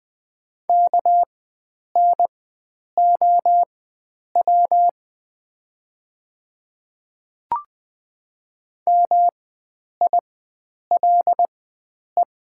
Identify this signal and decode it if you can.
Morse code, heard as a keyed tone, about 20 wpm, spaced as for 9 wpm, 700 Hz. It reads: KNOW MILE
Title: 4x Character Spacing - Top 500 Words - 20wpm